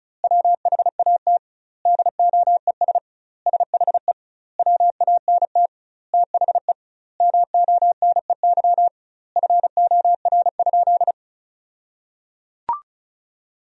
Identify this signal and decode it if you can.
Morse code, heard as a keyed tone, about 34 wpm, 700 Hz. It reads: WHAT DOES SHE WANT THE MONEY FOR?
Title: 2x Word-Spacing - Sentences from Top 400 Words - 35wpm